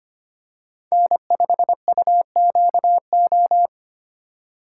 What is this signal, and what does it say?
Morse code, heard as a keyed tone, about 25 wpm, 700 Hz. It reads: N5UQO
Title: USA Call Signs - 5 Characters - 2x Character Spacing - 25wpm